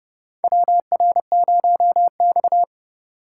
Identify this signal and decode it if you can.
Morse code, heard as a keyed tone, about 30 wpm, 700 Hz. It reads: WR0X
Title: USA Call Signs - 4 Characters - 3x Character Spacing - 30wpm